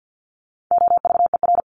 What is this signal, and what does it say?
Morse code, heard as a keyed tone, about 48 wpm, 700 Hz. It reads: OVER
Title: Top 300 Words 50wpm